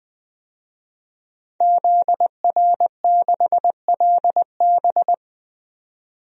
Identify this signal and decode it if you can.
Morse code, heard as a keyed tone, about 20 wpm, 700 Hz. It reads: ZR6LB
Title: DXCC-100 Call Signs - 5 Characters - 20wpm